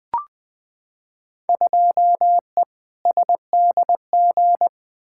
Morse code, heard as a keyed tone, about 20 wpm, 700 Hz. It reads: 2E SDG
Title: Field Day Exchanges - 20wpm